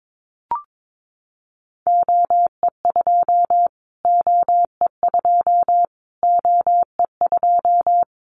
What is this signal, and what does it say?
Morse code, heard as a keyed tone, about 22 wpm, 700 Hz. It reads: OE2 OE2 OE2
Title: Callsign Prefixes for Top 25 DXCC - Repeated 3x - 22wpm